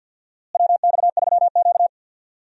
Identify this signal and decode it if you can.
Morse code, heard as a keyed tone, about 52 wpm, 700 Hz. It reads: WX3Q
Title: USA Call Signs - 4 Characters - Repeated 2x - 50wpm